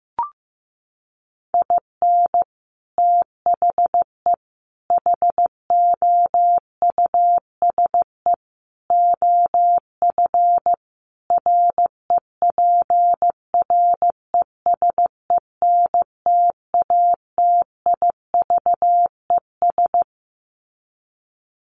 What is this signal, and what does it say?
Morse code, heard as a keyed tone, about 15 wpm, 700 Hz. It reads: IN THE HOUSE OF REPRESENTATIVES
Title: Sets of 5 Words Encore 15wpm